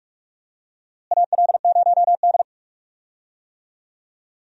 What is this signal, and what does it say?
Morse code, heard as a keyed tone, about 46 wpm, 700 Hz. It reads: AL0D
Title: USA Call Signs - 4 Characters - Repeated 2x - 45wpm